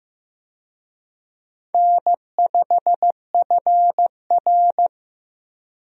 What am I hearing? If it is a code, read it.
Morse code, heard as a keyed tone, about 15 wpm, 700 Hz. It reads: N5FR
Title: USA Call Signs - 4 Characters - Repeated 2x - 15wpm